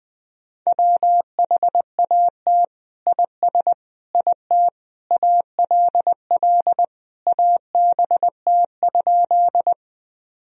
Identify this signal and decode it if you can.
Morse code, heard as a keyed tone, about 20 wpm, 700 Hz. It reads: WHAT IS IT ALL ABT?
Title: Sentences from Top 100 Words and Abbreviations 20wpm